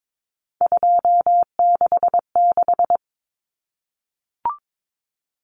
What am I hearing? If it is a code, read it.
Morse code, heard as a keyed tone, about 22 wpm, 700 Hz. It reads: 266
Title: Common Cut Numbers - 3-Digit 22wpm